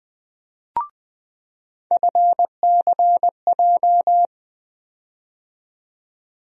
Morse code, heard as a keyed tone, about 20 wpm, 700 Hz. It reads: FCJ
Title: USA Call Sign Postfixes - 3 Letters 20wpm